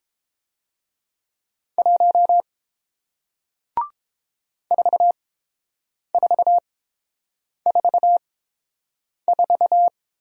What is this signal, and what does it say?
Morse code, heard as a keyed tone, about 30 wpm, spaced as for 7 wpm, 700 Hz. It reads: 1 4444